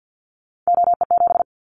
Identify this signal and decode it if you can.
Morse code, heard as a keyed tone, about 50 wpm, 700 Hz. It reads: OE7
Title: Callsign Prefixes for Top 25 DXCC - Repeated 3x - 50wpm